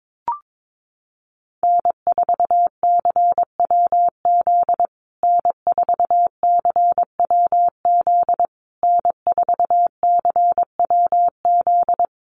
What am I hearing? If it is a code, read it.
Morse code, heard as a keyed tone, about 22 wpm, 700 Hz. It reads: N4CWZ N4CWZ N4CWZ
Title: USA Call Signs Repeated 3x 22wpm